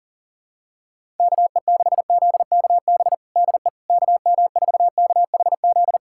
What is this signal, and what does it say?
Morse code, heard as a keyed tone, about 40 wpm, 700 Hz. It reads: KE6ZKB DE KM4KHZ
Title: USA Callsign DE Callsign 40wpm